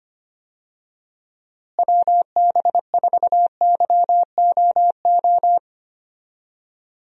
Morse code, heard as a keyed tone, about 25 wpm, 700 Hz. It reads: WB4YOO
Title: USA Call Signs - 6 Characters - Repeated 3x - 25wpm